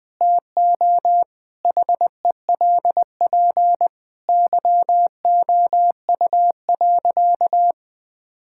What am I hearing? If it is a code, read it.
Morse code, heard as a keyed tone, about 20 wpm, 700 Hz. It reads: TO HELP YOU.
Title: Sentences with 4-Word Phrases - 20wpm